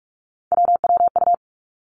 Morse code, heard as a keyed tone, about 46 wpm, 700 Hz. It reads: WWU
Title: Three Challenging Letter-Number Combinations 45wpm